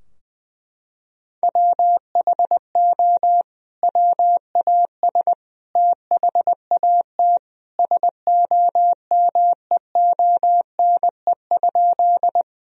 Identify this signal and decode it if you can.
Morse code, heard as a keyed tone, about 20 wpm, 700 Hz. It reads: WHO WAS THAT SOMEONE?
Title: Sentences from Top 1000 Words - Review Part 4 - 20wpm